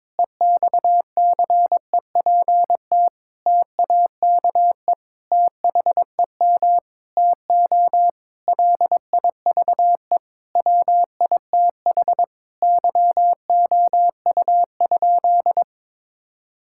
Morse code, heard as a keyed tone, about 22 wpm, 700 Hz. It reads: EXCEPT TAKE THEM TO LIVE WITH YOU?
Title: Sentences from Top 1600 Words 22wpm